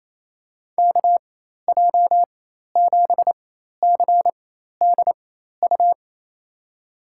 Morse code, heard as a keyed tone, about 28 wpm, spaced as for 13 wpm, 700 Hz. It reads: KJ7CDU
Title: USA Call Signs - 6 Characters - 4x Character Spacing - 28wpm